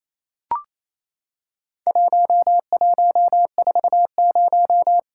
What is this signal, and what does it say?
Morse code, heard as a keyed tone, about 28 wpm, 700 Hz. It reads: 1140
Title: Worked All Europe DX Contest - QTC Exchange - 28wpm